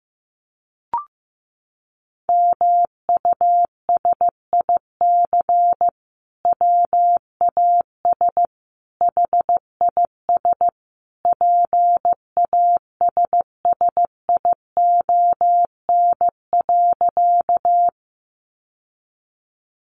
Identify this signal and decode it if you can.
Morse code, heard as a keyed tone, about 15 wpm, 700 Hz. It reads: MUSIC WAS HIS PASSION.
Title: Sentences from Top 1300 Words 15wpm